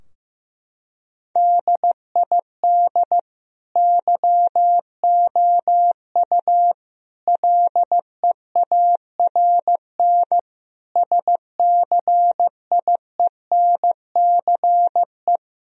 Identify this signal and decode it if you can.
Morse code, heard as a keyed tone, about 15 wpm, 700 Hz. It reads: DID YOU LEARN SCIENCE
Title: Sentences from Top 1000 Words - Review Part 2 - 15wpm